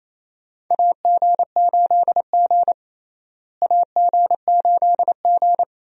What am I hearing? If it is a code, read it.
Morse code, heard as a keyed tone, about 28 wpm, 700 Hz. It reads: AG8G AG8G
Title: USA Call Signs - 4 Characters - Repeated 3x - 28wpm